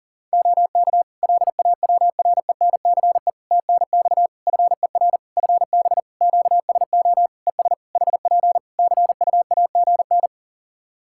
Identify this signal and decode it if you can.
Morse code, heard as a keyed tone, about 40 wpm, 700 Hz. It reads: OK LAWRENCE TNX FER FB QSO ES HP CUAGN
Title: Practice QSO - Exchange 4 - Ending - 40wpm